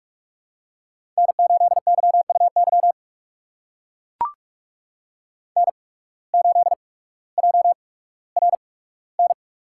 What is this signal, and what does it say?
Morse code, heard as a keyed tone, about 44 wpm, spaced as for 19 wpm, 700 Hz. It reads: N8YUY N8JRN
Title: USA Call Signs - 8x Character Spacing - 45wpm